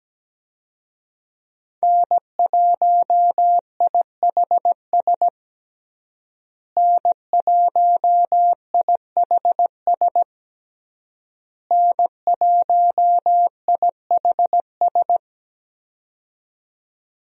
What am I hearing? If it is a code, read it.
Morse code, heard as a keyed tone, about 17 wpm, 700 Hz. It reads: N1IHS N1IHS N1IHS
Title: USA Call Signs - 5 Characters - Repeated 3x - 17wpm